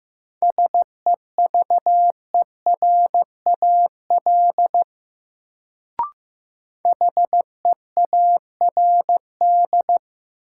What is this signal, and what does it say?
Morse code, heard as a keyed tone, about 15 wpm, 700 Hz. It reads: SEVERAL HEARD
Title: Sentences from Top 400 Words - New Word Drill - 15wpm